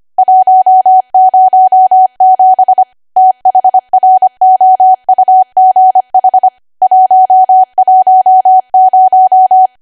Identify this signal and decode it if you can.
Morse code, heard as a keyed tone, about 25 wpm, 750 Hz. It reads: 107 THROUGH 110
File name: cwa / W1AW-Level3 / 25WPM